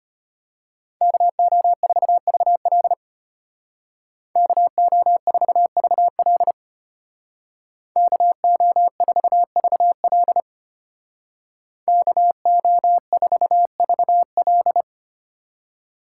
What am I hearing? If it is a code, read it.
Morse code, heard as a keyed tone, about 33 wpm, 700 Hz. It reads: KO4VL KO4VL KO4VL KO4VL